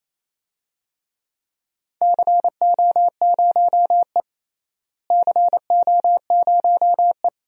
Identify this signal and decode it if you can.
Morse code, heard as a keyed tone, about 28 wpm, 700 Hz. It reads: CO0E CO0E
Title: DXCC-100 Call Signs - 4 Characters - Repeated 2x - 28wpm